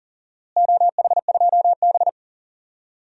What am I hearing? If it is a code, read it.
Morse code, heard as a keyed tone, about 40 wpm, 700 Hz. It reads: OH2B
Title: DXCC-100 Call Signs - 4 Characters - Repeated 2x - 40wpm